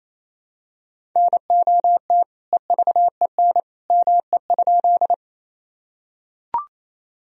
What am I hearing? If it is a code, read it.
Morse code, heard as a keyed tone, about 28 wpm, 700 Hz. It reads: NOT EVEN ME?
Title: Sentences from Top 1000 Words - Review Part 1 - 28wpm